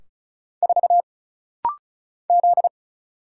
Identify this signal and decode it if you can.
Morse code, heard as a keyed tone, about 35 wpm, 700 Hz. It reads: 4 Z
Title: Single Letter-Number Alt - Rapid-Fire 35wpm